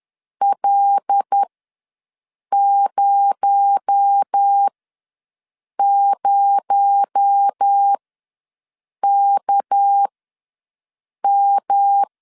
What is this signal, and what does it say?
Morse code, heard as a keyed tone, about 10 wpm, spaced as for 6 wpm, 800 Hz. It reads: L00KM
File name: LEZIONI / 6 PKX0 / 6 PKX0 ws 10wpm 4220